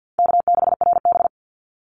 Morse code, heard as a keyed tone, about 50 wpm, 700 Hz. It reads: K6RB
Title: Easy Call Signs Repeated 2x - 50wpm